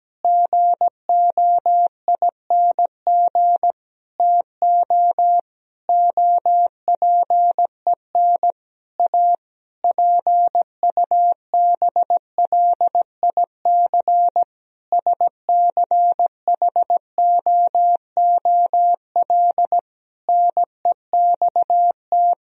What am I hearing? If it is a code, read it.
Morse code, heard as a keyed tone, about 17 wpm, 700 Hz. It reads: GOING TO OPEN A PUBLIC SCHOOL NEXT
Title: Sentences from Top 1000 Words - Review Part 3 - 17wpm